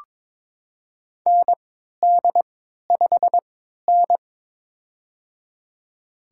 Morse code, heard as a keyed tone, about 21 wpm, spaced as for 12 wpm, 700 Hz. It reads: ND5N
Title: USA Call Signs - 4 Characters - 3x Character Spacing - 22wpm